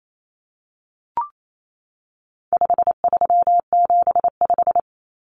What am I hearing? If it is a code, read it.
Morse code, heard as a keyed tone, about 28 wpm, 700 Hz. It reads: 5375
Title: Common Cut Numbers - 4-Digit 28wpm